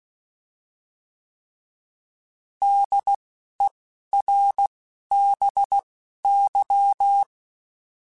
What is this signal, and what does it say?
Morse code, heard as a keyed tone, about 16 wpm, spaced as for 11 wpm, 800 Hz. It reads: DERBY